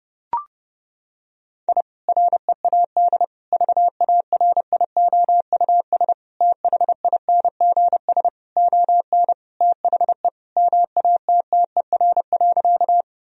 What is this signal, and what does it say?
Morse code, heard as a keyed tone, about 30 wpm, 700 Hz. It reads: I READ VARIOUS THINGS ON THE MATTER.
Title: Sentences from Top 1000 Words - Review Part 4 - 30wpm